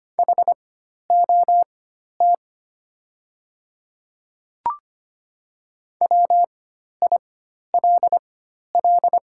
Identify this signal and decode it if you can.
Morse code, heard as a keyed tone, about 25 wpm, spaced as for 12 wpm, 700 Hz. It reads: HOT WILL
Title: 4x Character Spacing - Top 100 Words - 25wpm